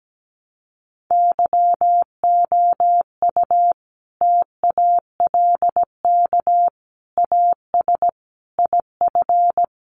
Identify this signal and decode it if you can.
Morse code, heard as a keyed tone, about 17 wpm, 700 Hz. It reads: YOU TALK AS IF
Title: Sentences from Top 1100 Words 17wpm